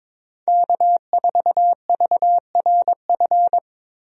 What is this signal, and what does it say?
Morse code, heard as a keyed tone, about 22 wpm, 700 Hz. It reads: K4VRF
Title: USA Call Signs - 5 Characters - Repeated 2x - 22wpm